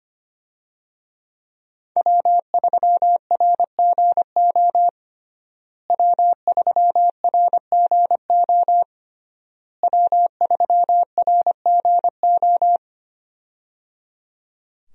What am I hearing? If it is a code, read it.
Morse code, heard as a keyed tone, about 25 wpm, 700 Hz. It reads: W3RGO W3RGO W3RGO
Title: USA Call Signs - 5 Characters - Repeated 3x - 25wpm